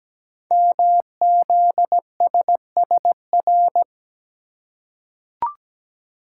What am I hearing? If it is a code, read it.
Morse code, heard as a keyed tone, about 17 wpm, 700 Hz. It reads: MZSSR
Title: Code Groups - 5 Letters - 17wpm